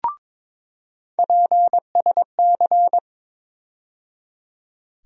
Morse code, heard as a keyed tone, about 22 wpm, 700 Hz. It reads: PSC
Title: USA Call Sign Postfixes - 3 Letters 22wpm